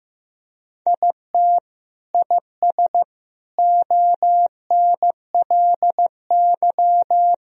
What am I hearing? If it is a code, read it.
Morse code, heard as a keyed tone, about 15 wpm, 700 Hz. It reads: IT IS ONLY
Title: Sentences from Top 1000 Words - Review Part 2 - 15wpm